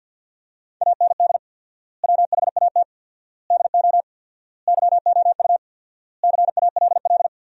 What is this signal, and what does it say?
Morse code, heard as a keyed tone, about 52 wpm, 700 Hz. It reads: AND WHAT DO YOU CALL
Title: 4x Word-Spacing - Sentences from Top 100 Words - 50wpm